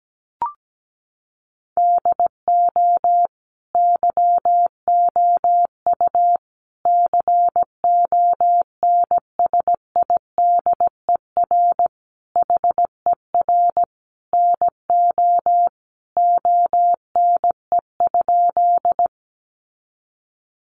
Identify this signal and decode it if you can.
Morse code, heard as a keyed tone, about 17 wpm, 700 Hz. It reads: DO YOU CONSIDER HER NO ONE?
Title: Sentences from Top 600 Words 17wpm